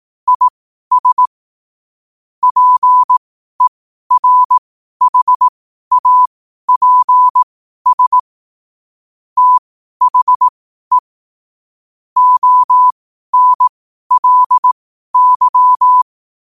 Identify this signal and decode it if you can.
Morse code, heard as a keyed tone, about 18 wpm, spaced as for 12 wpm, 1000 Hz. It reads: IS PERHAPS THE ONLY
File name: qotdfiles / 12wpm